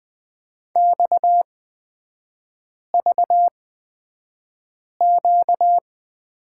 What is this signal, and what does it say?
Morse code, heard as a keyed tone, about 20 wpm, spaced as for 5 wpm, 700 Hz. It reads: XVQ
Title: 250 Lesson 40e - BK - Three Characters 20wpm